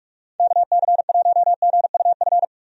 Morse code, heard as a keyed tone, about 46 wpm, 700 Hz. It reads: KC1GUF
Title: USA Call Signs - 6 Characters - 45wpm